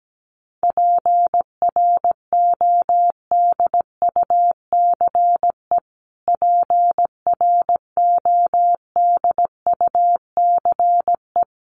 Morse code, heard as a keyed tone, about 17 wpm, 700 Hz. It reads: PRODUCE PRODUCE
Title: Top 500 Words Repeated 17wpm